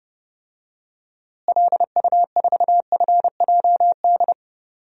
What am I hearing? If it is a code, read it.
Morse code, heard as a keyed tone, about 30 wpm, 700 Hz. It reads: LU4FJD
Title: DXCC-100 Call Signs - 6 Characters - Repeated 3x - 30wpm